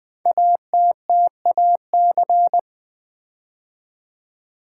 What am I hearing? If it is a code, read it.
Morse code, heard as a keyed tone, about 20 wpm, 700 Hz. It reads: ATTAC
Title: Build A Word - 5 Letters At A Time - 10-Letter Words - 20wpm